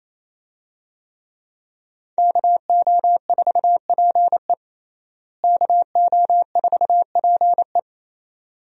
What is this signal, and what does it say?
Morse code, heard as a keyed tone, about 28 wpm, 700 Hz. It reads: KO4PE KO4PE
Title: USA Call Signs - 5 Characters - Repeated 2x - 28wpm